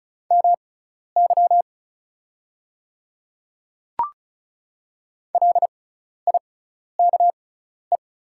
Morse code, heard as a keyed tone, about 36 wpm, spaced as for 12 wpm, 700 Hz. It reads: MY LIKE
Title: 6x Character Spacing - Top 100 Words - 35wpm